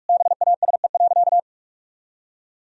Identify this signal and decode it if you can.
Morse code, heard as a keyed tone, about 41 wpm, 700 Hz. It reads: BASE.
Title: Sentences from Top 1000 Words - Review Part 1 - 45wpm